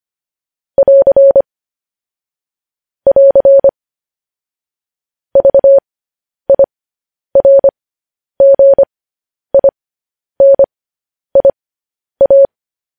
Morse code, heard as a keyed tone, about 25 wpm, spaced as for 10 wpm, 550 Hz. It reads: <AR> <AR> VIRGINIA
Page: news headlines in morse code at 10 wpm